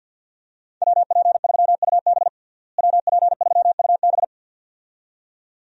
Morse code, heard as a keyed tone, about 53 wpm, 700 Hz. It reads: WP3UB WP3UB
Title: USA Call Signs - 5 Characters - Repeated 2x - 50wpm